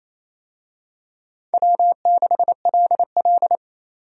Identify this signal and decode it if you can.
Morse code, heard as a keyed tone, about 27 wpm, 700 Hz. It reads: W6LL
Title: USA Call Signs - 4 Characters - 28wpm